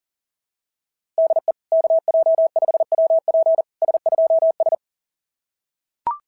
Morse code, heard as a keyed tone, about 40 wpm, 690 Hz. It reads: DE KJ5WP S2S
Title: SOTA QSO Practice 40wpm